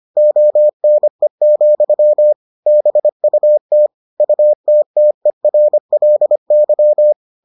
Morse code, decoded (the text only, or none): ONE, BUT UTTERLY